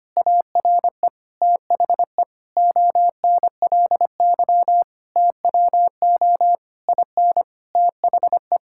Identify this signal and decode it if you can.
Morse code, heard as a keyed tone, about 25 wpm, 700 Hz. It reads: ARE THE ONLY TWO IN THE
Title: Sentences from Top 1000 Words - Review Part 6 - 25wpm